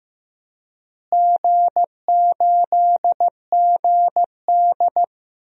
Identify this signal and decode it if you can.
Morse code, heard as a keyed tone, about 15 wpm, 700 Hz. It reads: G8GD
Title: DXCC-100 Call Signs - 4 Characters - Repeated 3x - 15wpm